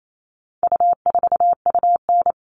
Morse code, heard as a keyed tone, about 28 wpm, 700 Hz. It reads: U4UN
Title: All Cut Numbers - 4-Digit 28wpm